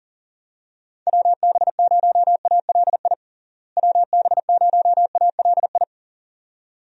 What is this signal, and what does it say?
Morse code, heard as a keyed tone, about 40 wpm, 700 Hz. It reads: WB0ALI WB0ALI
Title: USA Call Signs - 6 Characters - Repeated 2x - 40wpm